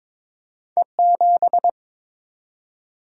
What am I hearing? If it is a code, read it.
Morse code, heard as a keyed tone, about 30 wpm, spaced as for 26 wpm, 700 Hz. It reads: E7
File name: Speed-Racing - Two Letter-Number Combinations - 22wpm